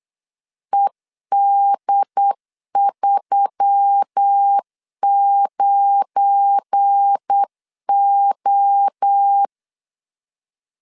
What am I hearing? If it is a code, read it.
Morse code, heard as a keyed tone, about 8 wpm, 800 Hz. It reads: ED39O
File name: CW / DET / 1 9 1500C S A 8wpm 5637